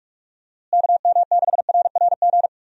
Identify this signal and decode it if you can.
Morse code, heard as a keyed tone, about 46 wpm, 700 Hz. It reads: KM6RRG
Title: USA Call Signs - 6 Characters - Repeated 2x - 45wpm